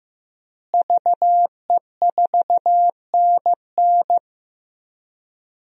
Morse code, heard as a keyed tone, about 15 wpm, 700 Hz. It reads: VE4NN
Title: DXCC-100 Call Signs - 5 Characters - Repeated 2x - 15wpm